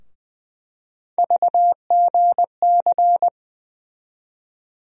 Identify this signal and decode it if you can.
Morse code, heard as a keyed tone, about 20 wpm, 700 Hz. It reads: VGC
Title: USA Call Sign Postfixes - 3 Letters 20wpm